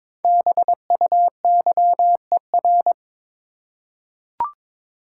Morse code, heard as a keyed tone, about 22 wpm, 700 Hz. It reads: BUYER
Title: Five Letter Words - 22wpm